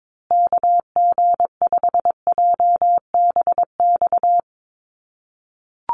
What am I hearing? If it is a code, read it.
Morse code, heard as a keyed tone, about 22 wpm, 700 Hz. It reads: KG5JBX
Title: USA Call Signs Repeated 3x 22wpm